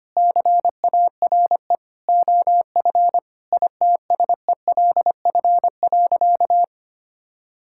Morse code, heard as a keyed tone, about 25 wpm, 700 Hz. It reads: CARE OF ITSELF.
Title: Sentences from Top 1000 Words - Review Part 5 - 25wpm